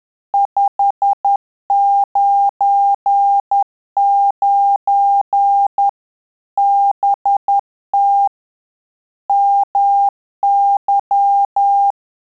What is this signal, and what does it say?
Morse code, heard as a keyed tone, about 10 wpm, 800 Hz. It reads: 599 BT MY
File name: Telegraphy-group / QSO CW 50 CPM